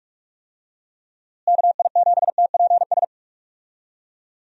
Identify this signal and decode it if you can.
Morse code, heard as a keyed tone, about 42 wpm, 700 Hz. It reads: KI7TPS
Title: USA Call Signs - 6 Characters - Repeated 2x - 45wpm